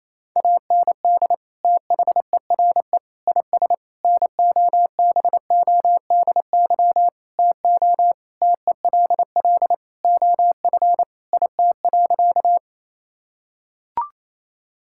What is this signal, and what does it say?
Morse code, heard as a keyed tone, about 28 wpm, 700 Hz. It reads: AND THERE IS NOBODY TO TELL OF IT.
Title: Sentences from Top 1600 Words 28wpm